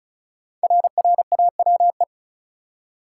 Morse code, heard as a keyed tone, about 36 wpm, 700 Hz. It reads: RRAWE
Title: Code Groups - 5 Letters - 35wpm